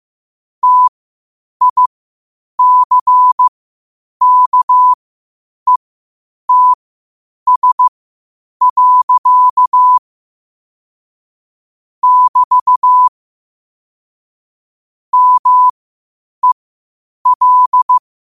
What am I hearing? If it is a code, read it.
Morse code, heard as a keyed tone, about 15 wpm, spaced as for 8 wpm, 1000 Hz. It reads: TICKETS. <BT> MEL
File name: qotdfiles / 8wpm